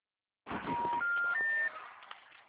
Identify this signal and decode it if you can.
Special information tone, with noise.